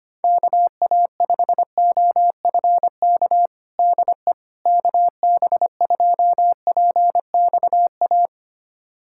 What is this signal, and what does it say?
Morse code, heard as a keyed tone, about 25 wpm, 700 Hz. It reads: KA5OFK DE KB2PXA
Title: USA Callsign DE Callsign - 6 Characters - 25wpm